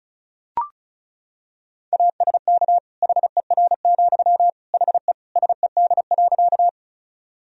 Morse code, heard as a keyed tone, about 36 wpm, 700 Hz. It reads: ASK HER, HE SED.
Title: Sentences from Top 200 Words and Abbreviations 35wpm